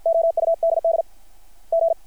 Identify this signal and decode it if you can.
Morse code, heard as a keyed tone, about 53 wpm, 660 Hz. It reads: OUDD G
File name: archived / CRSnet